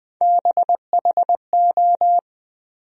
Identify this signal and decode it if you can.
Morse code, heard as a keyed tone, about 20 wpm, 700 Hz. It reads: BHO
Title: USA Call Sign Postfixes - 3 Letters 20wpm